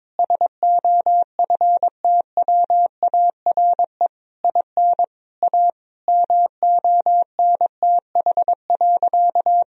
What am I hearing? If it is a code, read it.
Morse code, heard as a keyed tone, about 22 wpm, 700 Hz. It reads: SOFTWARE IN A MONTH.
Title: Sentences from Top 1000 Words - Review Part 3 - 22wpm